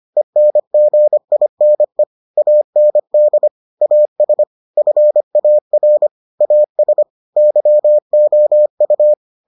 Morse code, heard as a keyed tone, about 25 wpm, 600 Hz. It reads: ENGINE AND AS FAR AS YOU